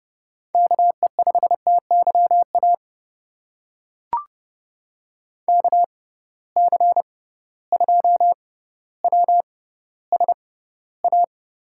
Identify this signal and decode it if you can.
Morse code, heard as a keyed tone, about 30 wpm, spaced as for 15 wpm, 700 Hz. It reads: KE5TYA KC2WSA
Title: USA Call Signs - 6x Character Spacing - 30wpm